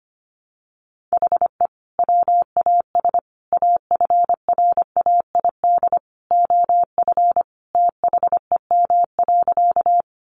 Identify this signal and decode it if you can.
Morse code, heard as a keyed tone, about 25 wpm, 700 Hz. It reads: HE WAS AFRAID OF THEM.
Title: Sentences from Top 1500 Words 25wpm